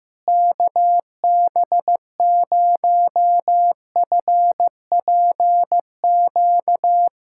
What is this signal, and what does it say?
Morse code, heard as a keyed tone, about 15 wpm, 700 Hz. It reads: KB0FPQ
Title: USA Callsign DE Callsign 15wpm